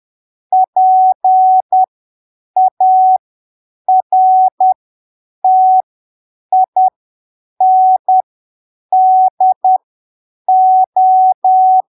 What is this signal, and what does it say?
Morse code, heard as a keyed tone, about 10 wpm, spaced as for 7 wpm, 750 Hz. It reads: PARTINDO
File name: arquivos / texto / texto 08 10ppm